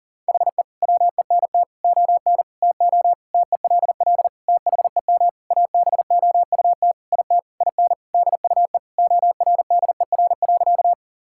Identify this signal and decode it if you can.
Morse code, heard as a keyed tone, about 37 wpm, 700 Hz. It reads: HE WENT ON TO TELL THEM ABOUT IT IN DUE ORDER.
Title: Sentences from Top 1000 Words - Review Part 6 - 40wpm